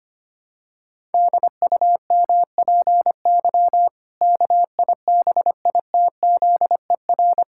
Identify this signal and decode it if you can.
Morse code, heard as a keyed tone, about 25 wpm, 700 Hz. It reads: DUMPY KIBITZER
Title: Pangram Challenge v2 - 25wpm